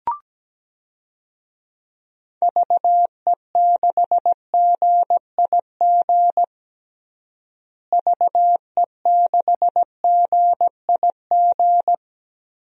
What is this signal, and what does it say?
Morse code, heard as a keyed tone, about 17 wpm, 700 Hz. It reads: VE6GIG VE6GIG
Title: DXCC-100 Call Signs - 6 Characters - Repeated 3x - 17wpm